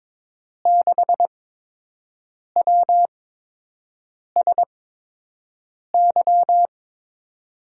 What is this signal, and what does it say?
Morse code, heard as a keyed tone, about 22 wpm, spaced as for 6 wpm, 700 Hz. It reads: 6WSY
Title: USA Call Signs - 6 Characters - 8x Character Spacing - 22wpm